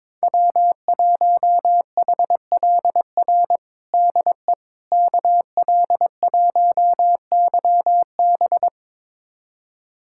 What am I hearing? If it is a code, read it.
Morse code, heard as a keyed tone, about 22 wpm, 700 Hz. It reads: W1HLR DE KL1YB
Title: USA Callsign DE Callsign - 5 Characters - 22wpm